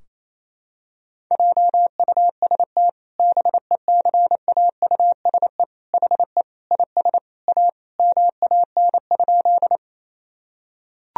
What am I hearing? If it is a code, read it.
Morse code, heard as a keyed tone, about 28 wpm, 700 Hz. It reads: JUST BECAUSE HE IS A MAN?
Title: Sentences from Top 1000 Words - Review Part 3 - 28wpm